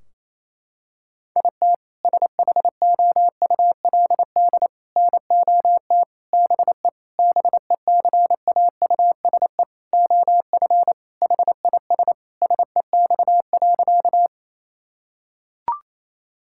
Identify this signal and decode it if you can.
Morse code, heard as a keyed tone, about 28 wpm, 700 Hz. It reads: IT SHOULD NOT BE BECAUSE OF HIS SEX.
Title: Sentences from Top 1600 Words 28wpm